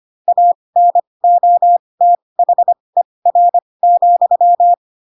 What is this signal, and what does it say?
Morse code, heard as a keyed tone, about 25 wpm, spaced as for 20 wpm, 700 Hz. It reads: ANOTHER,